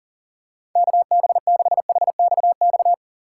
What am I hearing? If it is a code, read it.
Morse code, heard as a keyed tone, about 40 wpm, 700 Hz. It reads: KB6HXX
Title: USA Call Signs - 6 Characters - 2x Character Spacing - 40wpm